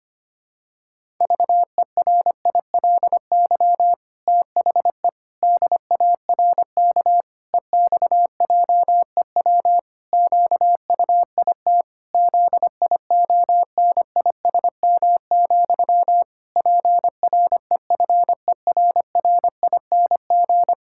Morse code, heard as a keyed tone, about 25 wpm, 700 Hz. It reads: VERILY THE DARK EXJEW QUIT ZIONISM, PREFERRING
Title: Pangram Challenge v2 - 25wpm